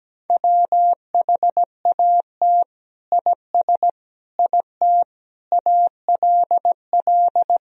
Morse code, heard as a keyed tone, about 17 wpm, 700 Hz. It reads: WHAT IS IT ALL
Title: Sentences from Top 100 Words and Abbreviations 17wpm